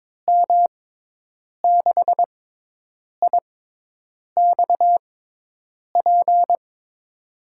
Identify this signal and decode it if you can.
Morse code, heard as a keyed tone, about 22 wpm, spaced as for 8 wpm, 700 Hz. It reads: M6IXP
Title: USA Call Signs - 6x Character Spacing - 22wpm